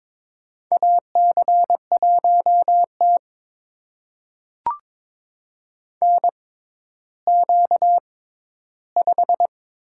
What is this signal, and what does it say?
Morse code, heard as a keyed tone, about 23 wpm, spaced as for 13 wpm, 700 Hz. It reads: AC1T NQ5
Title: USA Call Signs - 4 Characters - 6x Character Spacing - 22wpm